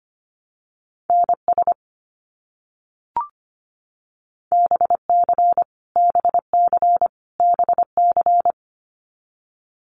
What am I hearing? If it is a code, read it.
Morse code, heard as a keyed tone, about 25 wpm, 700 Hz. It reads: NS BC BC BC
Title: Canadian Provinces and Territories - Repeated 3x - 25wpm